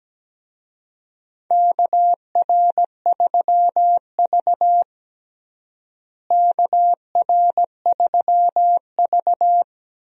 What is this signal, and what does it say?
Morse code, heard as a keyed tone, about 17 wpm, 700 Hz. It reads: KR3V KR3V
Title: USA Call Signs - 4 Characters - Repeated 2x - 17wpm